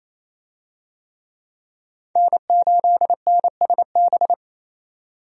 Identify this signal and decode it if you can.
Morse code, heard as a keyed tone, about 27 wpm, 700 Hz. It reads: N8NSB
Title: USA Call Signs - 5 Characters - 28wpm